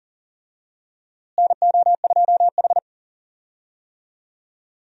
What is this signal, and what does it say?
Morse code, heard as a keyed tone, about 40 wpm, 700 Hz. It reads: NO2H
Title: USA Call Signs - 4 Characters - Repeated 2x - 40wpm